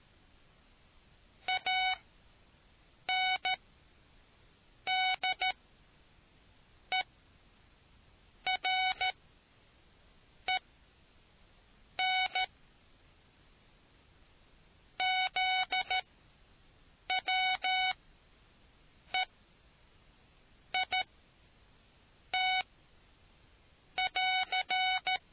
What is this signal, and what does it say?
Morse code, heard as a keyed tone, about 13 wpm, spaced as for 5 wpm, 750 Hz. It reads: ANDEREN ZWEIT <AR>